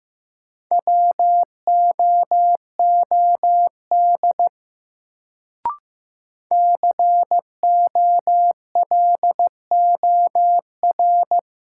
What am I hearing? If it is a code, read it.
Morse code, heard as a keyed tone, about 15 wpm, 700 Hz. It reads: WOOD COLOR
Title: Sentences from Top 1000 Words - Review Part 2 - New Word Drill - 15wpm